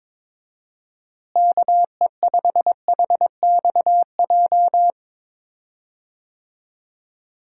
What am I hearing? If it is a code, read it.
Morse code, heard as a keyed tone, about 23 wpm, 700 Hz. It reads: KE5HXJ
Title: USA Call Signs - 6 Characters - Repeated 2x - 22wpm